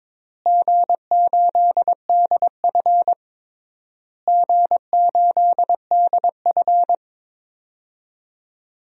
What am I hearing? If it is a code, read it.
Morse code, heard as a keyed tone, about 22 wpm, 700 Hz. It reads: G8DF G8DF
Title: DXCC-100 Call Signs - 4 Characters - Repeated 3x - 22wpm